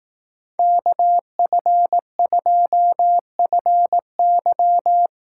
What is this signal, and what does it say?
Morse code, heard as a keyed tone, about 18 wpm, 700 Hz. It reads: KF2FY
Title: Speed Racing - USA 2x2 Call Signs 15wpm